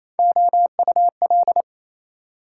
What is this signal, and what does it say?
Morse code, heard as a keyed tone, about 37 wpm, 700 Hz. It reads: OUL